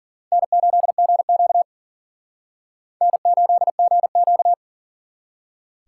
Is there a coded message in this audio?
Yes: Morse code from 0.3 to 4.5 s, heard as a keyed tone, about 52 wpm, 700 Hz. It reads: N8GQ N8GQ